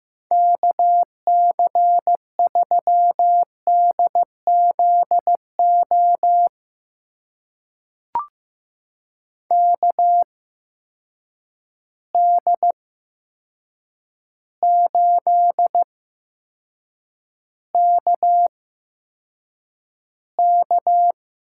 Morse code, heard as a keyed tone, about 15 wpm, spaced as for 7 wpm, 700 Hz. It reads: KC3DZO KD8KK
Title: USA Call Signs - 6 Characters - 8x Character Spacing - 15wpm